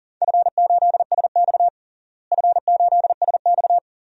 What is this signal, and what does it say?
Morse code, heard as a keyed tone, about 37 wpm, 700 Hz. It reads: F8SX F8SX
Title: DXCC-100 Call Signs - 4 Characters - Repeated 2x - 40wpm